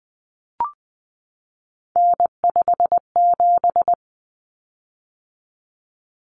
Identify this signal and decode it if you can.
Morse code, heard as a keyed tone, about 20 wpm, 700 Hz. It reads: N57
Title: Common Cut Numbers - 3-Digit 20wpm